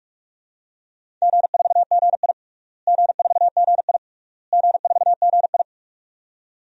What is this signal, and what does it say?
Morse code, heard as a keyed tone, about 46 wpm, 700 Hz. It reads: G4GI G4GI G4GI
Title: DXCC-100 Call Signs - 4 Characters - Repeated 3x - 45wpm